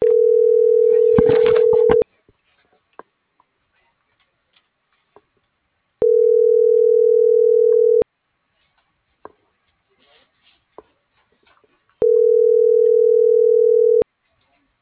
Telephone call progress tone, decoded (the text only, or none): ringback tone